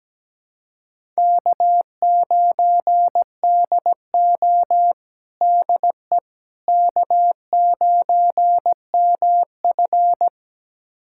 Morse code, heard as a keyed tone, about 17 wpm, 700 Hz. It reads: K9DO DE K9MF
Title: USA Callsign DE Callsign - 4 Characters - 17wpm